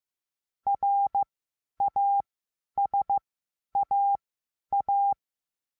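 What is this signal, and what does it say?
Morse code, heard as a keyed tone, about 15 wpm, spaced as for 10 wpm, 800 Hz. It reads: RASAA
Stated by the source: Morse lesson adding the letter A, sending random five-letter practice groups.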